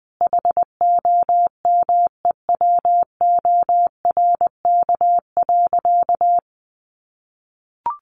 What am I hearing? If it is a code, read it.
Morse code, heard as a keyed tone, about 20 wpm, 700 Hz. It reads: HOMEWORK.
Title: Additional Easy Phrases 20wpm